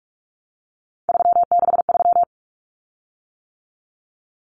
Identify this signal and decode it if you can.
Morse code, heard as a keyed tone, about 44 wpm, 700 Hz. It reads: 363
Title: Common Cut Numbers - 3-Digit 45wpm